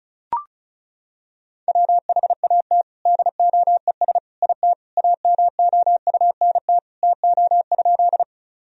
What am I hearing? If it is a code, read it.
Morse code, heard as a keyed tone, about 36 wpm, 700 Hz. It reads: WHAT DOES IT AMOUNT TO?
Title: Sentences from Top 1000 Words - Review Part 4 - 35wpm